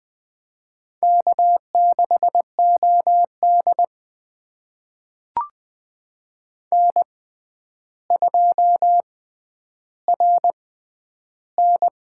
Morse code, heard as a keyed tone, about 20 wpm, spaced as for 10 wpm, 700 Hz. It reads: K6OD N2RN
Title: USA Call Signs - 4 Characters - 6x Character Spacing - 20wpm